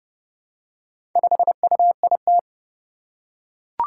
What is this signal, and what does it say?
Morse code, heard as a keyed tone, about 30 wpm, 700 Hz. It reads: 5UIT
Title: Four Challenging Letter-Number Combinations 30wpm